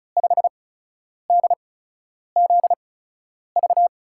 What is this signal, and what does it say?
Morse code, heard as a keyed tone, about 36 wpm, spaced as for 10 wpm, 700 Hz. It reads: 5DZV
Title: USA Call Signs - 6 Characters - 8x Character Spacing - 35wpm